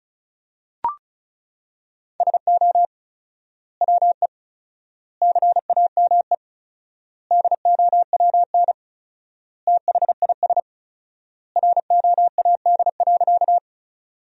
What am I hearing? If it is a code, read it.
Morse code, heard as a keyed tone, about 36 wpm, 700 Hz. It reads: SO WE CAME DOWN THIS ROAD.
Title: 4x Word-Spacing - Sentences from Top 400 Words - 35wpm